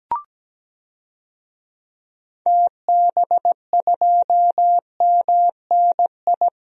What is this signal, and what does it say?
Morse code, heard as a keyed tone, about 17 wpm, 700 Hz. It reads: TB2MNI
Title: DXCC-100 Call Signs - 6 Characters - Repeated 2x - 17wpm